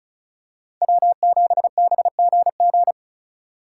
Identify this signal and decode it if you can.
Morse code, heard as a keyed tone, about 36 wpm, 700 Hz. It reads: W7BGG
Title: USA Call Signs - 2x Character Spacing - 35wpm